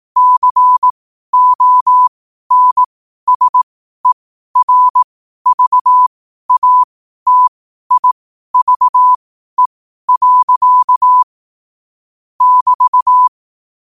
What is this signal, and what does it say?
Morse code, heard as a keyed tone, about 18 wpm, spaced as for 12 wpm, 1000 Hz. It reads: CONSERVATIVE. <BT>
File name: qotdfiles / 12wpm